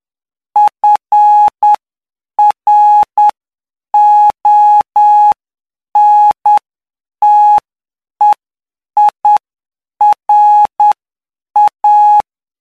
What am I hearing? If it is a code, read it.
Morse code, heard as a keyed tone, about 9 wpm, spaced as for 7 wpm, 840 Hz. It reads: FRONTEIRA
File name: cw / py2lcd / 09-PY2LCD-Li��o 18